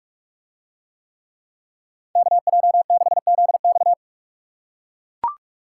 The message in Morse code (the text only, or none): KJ6ZX